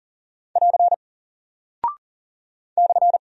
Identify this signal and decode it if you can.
Morse code, heard as a keyed tone, about 40 wpm, 700 Hz. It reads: <AR> /
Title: Single Character (Letter:Number:Punctuation:Prosign) 40wpm